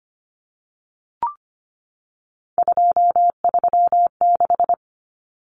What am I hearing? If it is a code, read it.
Morse code, heard as a keyed tone, about 25 wpm, 700 Hz. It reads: 236
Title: Numbers - 3-Digit 25wpm